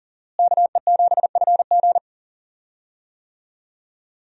Morse code, heard as a keyed tone, about 37 wpm, 700 Hz. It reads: KE7FG
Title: USA Call Signs - 5 Characters - Repeated 2x - 40wpm